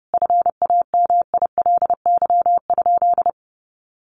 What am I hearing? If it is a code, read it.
Morse code, heard as a keyed tone, about 30 wpm, 700 Hz. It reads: FAMILY?
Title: Sentences from Top 1400 Words 30wpm